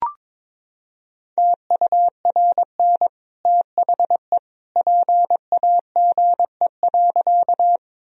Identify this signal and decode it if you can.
Morse code, heard as a keyed tone, about 22 wpm, 700 Hz. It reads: TURN THE PAGE.
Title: Sentences from Top 1000 Words - Review Part 1 - 22wpm